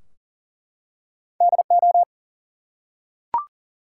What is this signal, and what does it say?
Morse code, heard as a keyed tone, about 40 wpm, 700 Hz. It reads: DO
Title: Two Letter Words - 40wpm